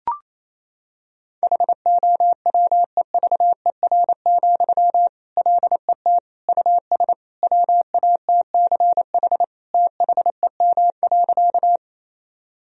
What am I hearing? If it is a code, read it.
Morse code, heard as a keyed tone, about 28 wpm, 700 Hz. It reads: HOWEVER, LET US WATCH THEM.
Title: Sentences from Top 1000 Words - Review Part 1 - 28wpm